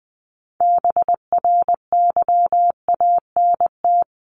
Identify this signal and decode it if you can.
Morse code, heard as a keyed tone, about 20 wpm, 700 Hz. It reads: BRYANT
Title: CWT Exchange 20wpm